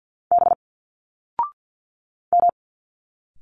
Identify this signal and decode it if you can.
Morse code, heard as a keyed tone, about 50 wpm, 700 Hz. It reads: B M
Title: Single Letters 50wpm